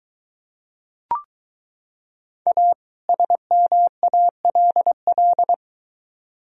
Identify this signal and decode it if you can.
Morse code, heard as a keyed tone, about 22 wpm, 700 Hz. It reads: A SMALL